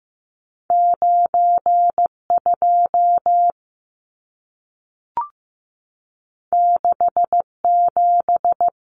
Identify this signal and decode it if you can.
Morse code, heard as a keyed tone, about 15 wpm, 700 Hz. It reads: 92 67
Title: Numbers - 2-Digit 15wpm